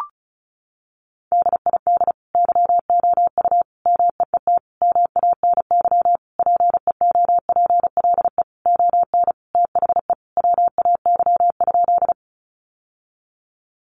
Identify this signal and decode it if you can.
Morse code, heard as a keyed tone, about 34 wpm, 700 Hz. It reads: DID YOU MEET MANY PEOPLE ON THE WAY?
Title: Sentences from Top 1100 Words 35wpm